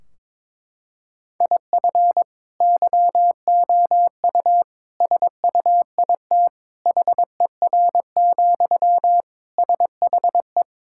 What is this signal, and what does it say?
Morse code, heard as a keyed tone, about 22 wpm, 700 Hz. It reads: IF YOU SUIT HER, SHE
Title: Sentences from Top 1600 Words 22wpm